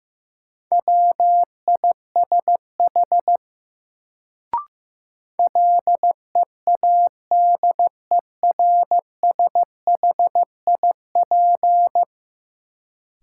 Morse code, heard as a keyed tone, about 15 wpm, 700 Hz. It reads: WISH LEADERSHIP
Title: Sentences from Top 1600 Words - New Word Drill Alt - 15wpm